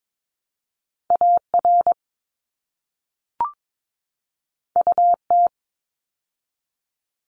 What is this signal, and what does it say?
Morse code, heard as a keyed tone, about 22 wpm, 700 Hz. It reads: AR UT
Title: ARRL Sections 22wpm